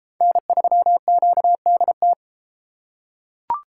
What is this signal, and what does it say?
Morse code, heard as a keyed tone, about 32 wpm, 700 Hz. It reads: N3QDT